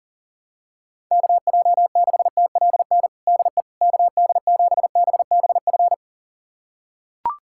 Morse code, heard as a keyed tone, about 40 wpm, 700 Hz. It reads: KJ6TLN DE KD7BBF
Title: USA Callsign DE Callsign - 6 Characters - 40wpm